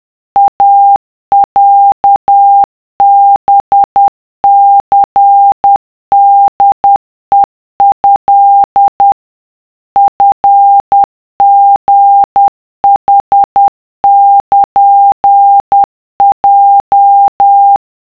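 Morse code (Japanese Wabun code), heard as a keyed tone, about 10 wpm, 800 Hz. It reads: イロハニホヘト チリヌルヲ
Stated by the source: Japanese wabun Morse of the iroha poem, generated as a wav by a text-to-Morse app at 50 CPM (10 WPM).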